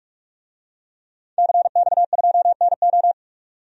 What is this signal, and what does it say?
Morse code, heard as a keyed tone, about 42 wpm, 700 Hz. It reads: CX2NO
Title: DXCC-100 Call Signs - 5 Characters - 45wpm